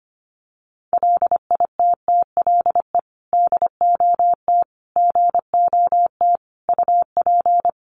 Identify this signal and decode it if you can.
Morse code, heard as a keyed tone, about 25 wpm, 700 Hz. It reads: LITTLE DOT GOT UP
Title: Sentences from Top 1500 Words 25wpm